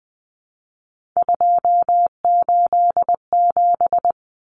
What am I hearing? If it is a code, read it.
Morse code, heard as a keyed tone, about 20 wpm, 700 Hz. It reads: 287
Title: Numbers - 3-Digit 20wpm